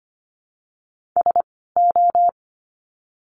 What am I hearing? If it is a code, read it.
Morse code, heard as a keyed tone, about 25 wpm, 700 Hz. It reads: S O
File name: course / 25:10wpm / 094 Lesson 16f - Letter c - Words 25wpm